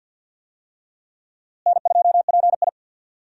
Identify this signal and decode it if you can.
Morse code, heard as a keyed tone, about 47 wpm, 700 Hz. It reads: N2PI
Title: USA Call Signs - 4 Characters - Repeated 3x - 50wpm